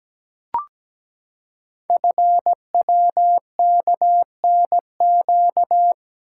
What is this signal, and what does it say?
Morse code, heard as a keyed tone, about 17 wpm, 700 Hz. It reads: FWKNQ
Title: Code Groups - 5 Letters - 17wpm